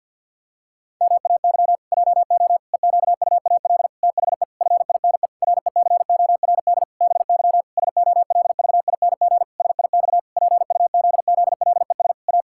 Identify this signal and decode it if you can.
Morse code, heard as a keyed tone, about 47 wpm, 700 Hz. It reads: MAY JO EQUAL THE FINE RECORD BY SOLVING SIX PUZZLES A
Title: Pangram Challenge v2 - 50wpm